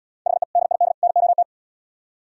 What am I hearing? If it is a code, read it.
Morse code, heard as a keyed tone, about 71 wpm, 700 Hz. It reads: HE DID NOT